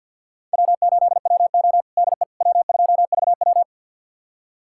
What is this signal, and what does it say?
Morse code, heard as a keyed tone, about 52 wpm, 690 Hz. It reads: W8WO DE W2VW
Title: USA Callsign DE Callsign - 4 Characters - 50wpm